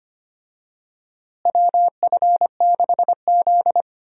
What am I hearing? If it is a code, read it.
Morse code, heard as a keyed tone, about 25 wpm, 700 Hz. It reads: WF6Z
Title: USA Call Signs - 4 Characters - Repeated 3x - 25wpm